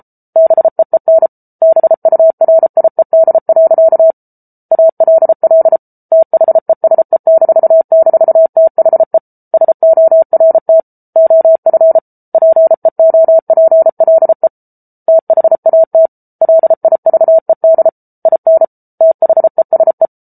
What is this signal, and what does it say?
Morse code, heard as a keyed tone, about 34 wpm, 650 Hz. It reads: BEEN BURIED. ALL THESE--THE SORT OF PEOPLE THAT LIVED IN THESE